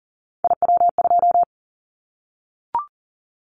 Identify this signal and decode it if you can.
Morse code, heard as a keyed tone, about 40 wpm, 700 Hz. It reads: IW2
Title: Callsign Prefixes for Top 50 DXCC - 40wpm